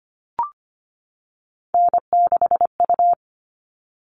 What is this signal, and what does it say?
Morse code, heard as a keyed tone, about 25 wpm, 700 Hz. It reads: N6U
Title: All Cut Numbers - 3-Digit 25wpm